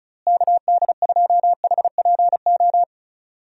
Morse code, heard as a keyed tone, about 36 wpm, 700 Hz. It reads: KD2HPO
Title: USA Call Signs - 6 Characters - Repeated 2x - 35wpm